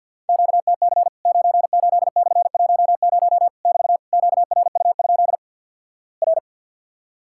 Morse code, heard as a keyed tone, about 50 wpm, 700 Hz. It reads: QTC 98/10 <BT> QRU? R
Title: Worked All Europe DX Contest - QTC Exchange - 50wpm